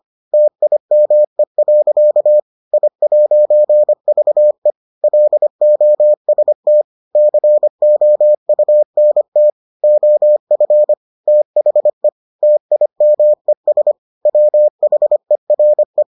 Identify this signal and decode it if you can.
Morse code, heard as a keyed tone, about 25 wpm, 600 Hz. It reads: TIME. I'VE LOST COUNT OF THE TIMES WHERE